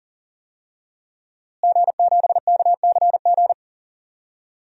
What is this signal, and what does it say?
Morse code, heard as a keyed tone, about 40 wpm, 700 Hz. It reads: G7KCG
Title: DXCC-100 Call Signs - 5 Characters - Repeated 3x - 40wpm